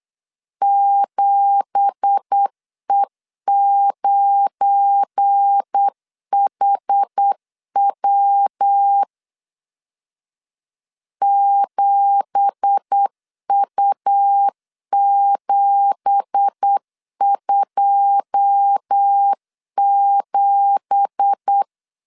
Morse code, heard as a keyed tone, about 8 wpm, 800 Hz. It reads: 7E9HW 7U727
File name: LEZIONI / 9 2867 8wpm 3440